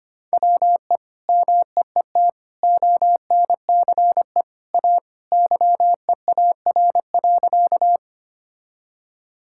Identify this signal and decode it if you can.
Morse code, heard as a keyed tone, about 25 wpm, 700 Hz. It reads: WE MEET ONCE A YEAR.
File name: practice / 25wpm / Sentences with 2-Word Phrases - 25wpm